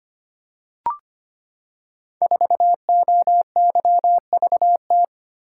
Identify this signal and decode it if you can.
Morse code, heard as a keyed tone, about 25 wpm, 700 Hz. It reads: 4OYVT
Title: Code Groups - 5 Letter:Numbers - 25wpm